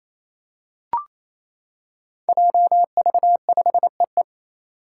Morse code, heard as a keyed tone, about 28 wpm, 700 Hz. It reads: JV5EE
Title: Five Challenging Letter-Number Combinations 28wpm